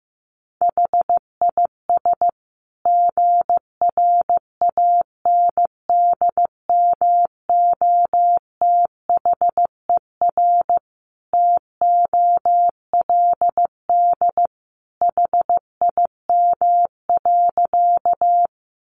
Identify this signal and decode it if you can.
Morse code, heard as a keyed tone, about 15 wpm, 700 Hz. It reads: HIS GRANDMOTHER TOLD HIM.
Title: Sentences from Top 1200 Words 15wpm